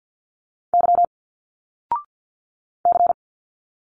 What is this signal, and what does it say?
Morse code, heard as a keyed tone, about 50 wpm, 700 Hz. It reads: Y C
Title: Single Letters 50wpm